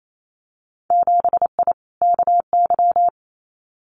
Morse code, heard as a keyed tone, about 28 wpm, 700 Hz. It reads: 7I KY
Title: Winter Field Day Exchanges Encore 28wpm